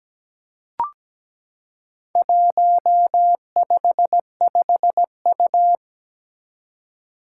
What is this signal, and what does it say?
Morse code, heard as a keyed tone, about 17 wpm, 700 Hz. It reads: 155U
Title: Four Challenging Letter-Number Combinations 17wpm